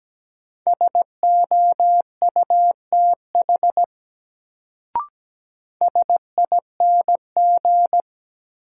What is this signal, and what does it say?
Morse code, heard as a keyed tone, about 17 wpm, 700 Hz. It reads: SOUTH SING
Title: Sentences from Top 400 Words - New Word Drill - 17wpm